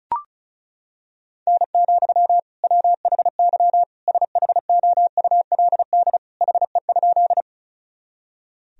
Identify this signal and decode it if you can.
Morse code, heard as a keyed tone, about 35 wpm, 700 Hz. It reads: N, WHY SHOULD HE?
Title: Sentences from Top 300 Words and Abbreviations 35wpm